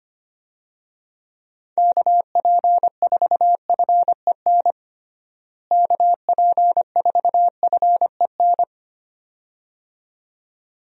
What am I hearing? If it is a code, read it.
Morse code, heard as a keyed tone, about 25 wpm, 700 Hz. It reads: KP4FEN KP4FEN
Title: DXCC-100 Call Signs - 6 Characters - Repeated 2x - 25wpm